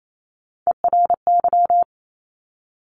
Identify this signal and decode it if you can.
Morse code, heard as a keyed tone, about 28 wpm, 700 Hz. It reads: ERY
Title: Top Three Letter Combinations in Words 28wpm